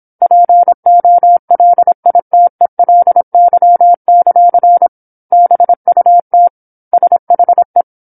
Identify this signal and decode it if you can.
Morse code, heard as a keyed tone, about 27 wpm, 700 Hz. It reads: POLITELY; BUT SHE
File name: 08-Chapter-08 26 WPM